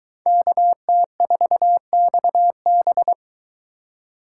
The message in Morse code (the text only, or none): KT4XB